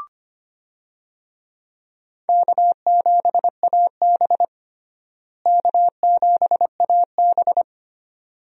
Morse code, heard as a keyed tone, about 25 wpm, 700 Hz. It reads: K7AB K7AB
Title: USA Call Signs - 4 Characters - Repeated 2x - 25wpm